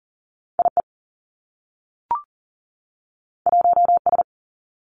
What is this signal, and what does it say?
Morse code, heard as a keyed tone, about 39 wpm, 700 Hz. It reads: I E 1 S
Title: Two Challenging Letter-Number Combinations 40wpm